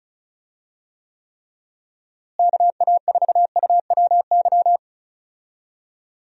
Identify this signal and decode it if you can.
Morse code, heard as a keyed tone, about 34 wpm, 700 Hz. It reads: KA4UWY
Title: USA Call Signs - 6 Characters - Repeated 3x - 35wpm